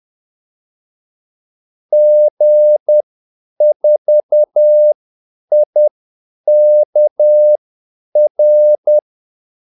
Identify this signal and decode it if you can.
Morse code, heard as a keyed tone, about 10 wpm, spaced as for 8 wpm, 600 Hz. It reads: G4IKR
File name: Callsigns-10-8wpm